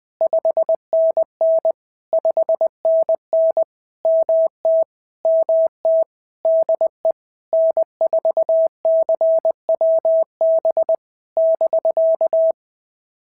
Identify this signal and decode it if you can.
Morse code, heard as a keyed tone, about 20 wpm, 650 Hz. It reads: 5NN 5NN MT MT DE N4CWB <BK>
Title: POTA QSO Practice - 20wpm